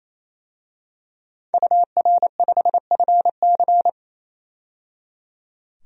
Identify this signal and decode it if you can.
Morse code, heard as a keyed tone, about 27 wpm, 700 Hz. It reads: UR5FC
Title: DXCC-100 Call Signs - 5 Characters - 28wpm